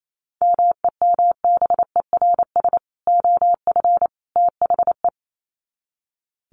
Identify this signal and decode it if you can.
Morse code, heard as a keyed tone, about 28 wpm, 700 Hz. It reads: MEMBERS OF THE
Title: Sets of 3 Words 28wpm